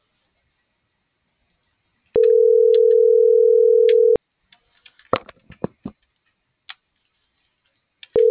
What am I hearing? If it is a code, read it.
Ringback tone (North America), 440 + 480 Hz.